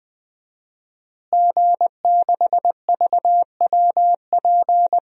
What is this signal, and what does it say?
Morse code, heard as a keyed tone, about 20 wpm, 700 Hz. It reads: G6VWP
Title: DXCC-100 Call Signs - 5 Characters - Repeated 2x - 20wpm